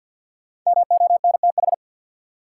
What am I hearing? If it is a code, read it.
Morse code, heard as a keyed tone, about 53 wpm, 700 Hz. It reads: MONTH
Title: Sets of 5 Words Encore - New Word Drill - 50wpm